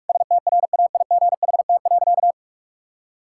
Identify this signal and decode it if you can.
Morse code, heard as a keyed tone, about 46 wpm, 700 Hz. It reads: STRAIGHT.
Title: Sentences from Top 1000 Words - Review Part 6 - 45wpm